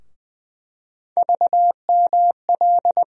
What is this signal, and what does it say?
Morse code, heard as a keyed tone, about 20 wpm, 700 Hz. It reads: VML
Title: USA Call Sign Postfixes - 3 Letters 20wpm